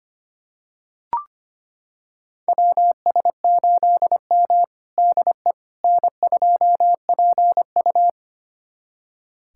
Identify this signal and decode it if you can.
Morse code, heard as a keyed tone, about 25 wpm, 700 Hz. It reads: WS8M DE N2PU
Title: USA Callsign DE Callsign - 4 Characters - 25wpm